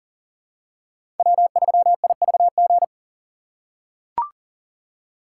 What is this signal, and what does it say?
Morse code, heard as a keyed tone, about 40 wpm, 700 Hz. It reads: W3IVG
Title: USA Call Signs - 2x Character Spacing - 40wpm